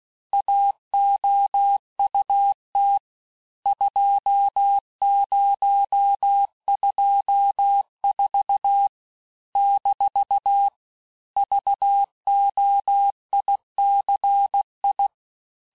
Morse code, heard as a keyed tone, about 16 wpm, 800 Hz. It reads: AOUT 2024 - VOICI
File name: cours-du-lundi files / cours-1592--teste-chiffre-vitesse-15mpm